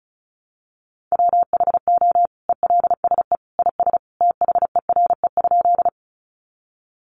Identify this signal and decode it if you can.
Morse code, heard as a keyed tone, about 34 wpm, 700 Hz. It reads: WHO ELSE IS THERE?
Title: Sentences from Top 800 Words 35wpm